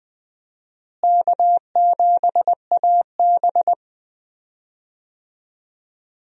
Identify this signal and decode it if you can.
Morse code, heard as a keyed tone, about 20 wpm, 700 Hz. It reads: K7AB
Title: USA Call Signs - 4 Characters - Repeated 2x - 20wpm